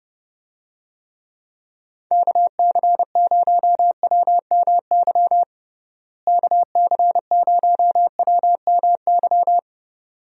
Morse code, heard as a keyed tone, about 28 wpm, 700 Hz. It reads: KC0WMY KC0WMY
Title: USA Call Signs - 6 Characters - Repeated 2x - 30wpm